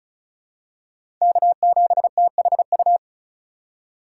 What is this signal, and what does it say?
Morse code, heard as a keyed tone, about 36 wpm, 700 Hz. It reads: K7THU
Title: USA Call Signs - 5 Characters - 4x Character Spacing - 35wpm